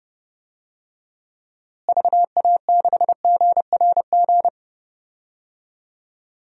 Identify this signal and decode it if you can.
Morse code, heard as a keyed tone, about 30 wpm, 700 Hz. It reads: VA6GRG
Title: DXCC-100 Call Signs - 6 Characters - Repeated 3x - 30wpm